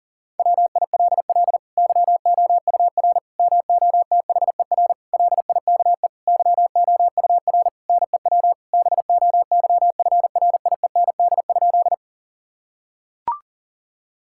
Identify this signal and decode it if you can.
Morse code, heard as a keyed tone, about 39 wpm, 700 Hz. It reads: WILL YOUR MOTHER LIKE YOUR NEW BOYFRIEND?
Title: Sentences from Top 1000 Words - Review Part 5 - 40wpm